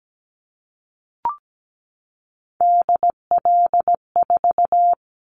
Morse code, heard as a keyed tone, about 17 wpm, 700 Hz. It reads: DL4
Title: Callsign Prefixes for Top 50 DXCC - 17wpm